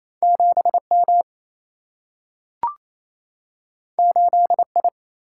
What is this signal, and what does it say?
Morse code, heard as a keyed tone, about 28 wpm, 700 Hz. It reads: 7 M 8 I
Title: Single-Letter Single-Number 28wpm